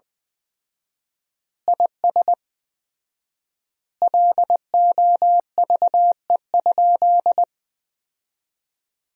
Morse code, heard as a keyed tone, about 20 wpm, 700 Hz. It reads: IS LOVE?
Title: 4x Word-Spacing - Sentences from Top 400 Words - 20wpm